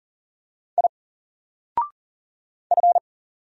Morse code, heard as a keyed tone, about 40 wpm, 700 Hz. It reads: I F
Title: Single Letter-Number 40wpm